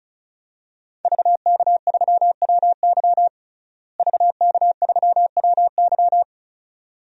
Morse code, heard as a keyed tone, about 36 wpm, 700 Hz. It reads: VK3WY VK3WY
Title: DXCC-100 Call Signs - 5 Characters - Repeated 2x - 35wpm